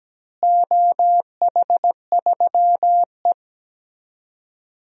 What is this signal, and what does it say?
Morse code, heard as a keyed tone, about 17 wpm, 700 Hz. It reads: OH3E
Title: DXCC-100 Call Signs - 4 Characters - Repeated 3x - 17wpm